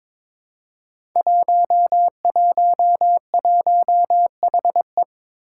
Morse code, heard as a keyed tone, about 22 wpm, 700 Hz. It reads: 111HE
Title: Five Challenging Letter-Number Combinations 22wpm